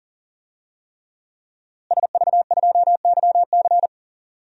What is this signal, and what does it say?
Morse code, heard as a keyed tone, about 40 wpm, 700 Hz. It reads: SV2YC
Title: DXCC-100 Call Signs - 5 Characters - Repeated 3x - 40wpm